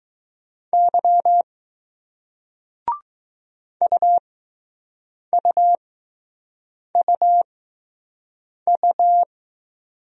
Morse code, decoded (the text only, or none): Y U U U U